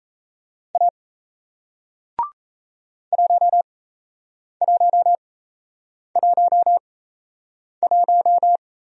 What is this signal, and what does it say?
Morse code, heard as a keyed tone, about 38 wpm, spaced as for 8 wpm, 700 Hz. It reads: A 1111